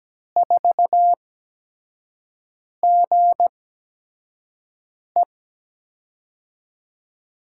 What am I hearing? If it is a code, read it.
Morse code, heard as a keyed tone, about 17 wpm, spaced as for 5 wpm, 700 Hz. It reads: 4GE
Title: USA Call Signs - 5 Characters - 8x Character Spacing - 17wpm